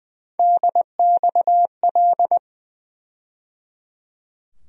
Morse code, heard as a keyed tone, about 20 wpm, 700 Hz. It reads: DXL
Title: USA Call Sign Postfixes - 3 Letters 20wpm